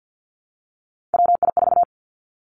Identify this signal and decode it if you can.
Morse code, heard as a keyed tone, about 50 wpm, 700 Hz. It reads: WI4
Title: Three Challenging Letter-Number Combinations 50wpm